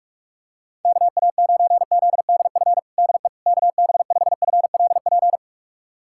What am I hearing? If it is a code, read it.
Morse code, heard as a keyed tone, about 44 wpm, 700 Hz. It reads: KA9ZDF DE KB5FLP
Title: USA Callsign DE Callsign - 6 Characters - 45wpm